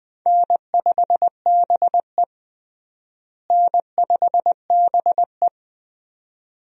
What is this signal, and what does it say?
Morse code, heard as a keyed tone, about 20 wpm, 700 Hz. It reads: N5BE N5BE
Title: USA Call Signs - 4 Characters - Repeated 2x - 20wpm